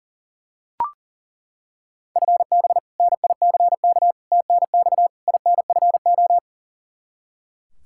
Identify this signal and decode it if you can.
Morse code, heard as a keyed tone, about 40 wpm, 700 Hz. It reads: FB NICK TNX INFO
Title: Practice QSO - Exchange 3 - More Info - 40wpm